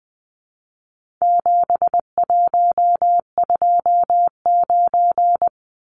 Morse code, heard as a keyed tone, about 20 wpm, 700 Hz. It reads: 7129
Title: Numbers - 4-Digit 20wpm